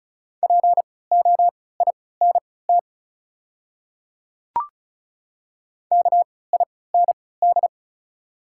Morse code, heard as a keyed tone, about 34 wpm, spaced as for 20 wpm, 700 Hz. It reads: POINT KIND
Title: 3x Character Spacing - Top 300 Words - 35wpm